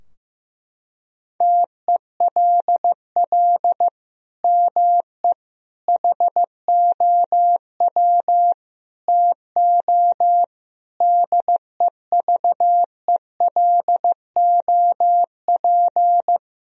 Morse code, heard as a keyed tone, about 15 wpm, 700 Hz. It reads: TELL ME HOW TO DEVELOP
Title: Sentences from Top 1000 Words - Review Part 2 - 15wpm